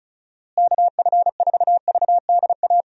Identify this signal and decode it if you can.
Morse code, heard as a keyed tone, about 36 wpm, 700 Hz. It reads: KF4VDA
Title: USA Call Signs - 6 Characters - Repeated 2x - 35wpm